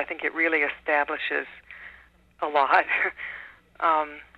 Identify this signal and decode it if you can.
Emergency Alert System (SAME) tones, heard as harsh buzzing bursts, with speech. It is heard from 0.2 to 3.5 s.